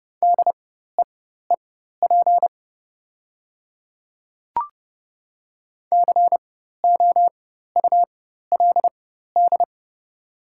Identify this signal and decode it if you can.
Morse code, heard as a keyed tone, about 30 wpm, spaced as for 14 wpm, 700 Hz. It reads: DEEP COULD
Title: 4x Character Spacing - Top 500 Words - 30wpm